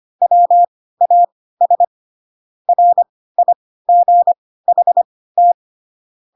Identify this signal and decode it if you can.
Morse code, heard as a keyed tone, about 25 wpm, spaced as for 16 wpm, 700 Hz. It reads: WAS RIGHT
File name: Pinocchio / Pinocchio ch7 10 25 ik1hge